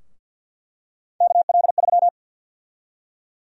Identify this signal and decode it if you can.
Morse code, heard as a keyed tone, about 50 wpm, 700 Hz. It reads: KL3
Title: USA Call Sign Prefixes - Repeated 2x - 50wpm